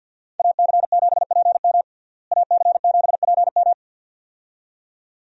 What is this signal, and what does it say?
Morse code, heard as a keyed tone, about 51 wpm, 700 Hz. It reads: AC7PM AC7PM
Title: USA Call Signs - 5 Characters - Repeated 2x - 50wpm